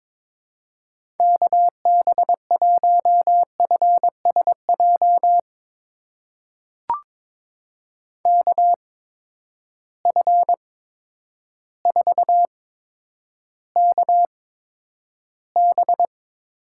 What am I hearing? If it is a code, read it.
Morse code, heard as a keyed tone, about 22 wpm, spaced as for 10 wpm, 700 Hz. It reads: KB1FSJ KF4KB
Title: USA Call Signs - 6 Characters - 8x Character Spacing - 22wpm